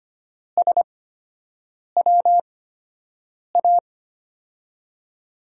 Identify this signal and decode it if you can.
Morse code, heard as a keyed tone, about 25 wpm, spaced as for 7 wpm, 700 Hz. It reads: SWA
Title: USA Call Signs - 6 Characters - 8x Character Spacing - 25wpm